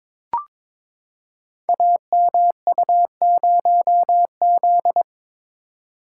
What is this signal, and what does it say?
Morse code, heard as a keyed tone, about 22 wpm, 700 Hz. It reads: AMU0Z
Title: Code Groups - 5 Letter:Numbers - 22wpm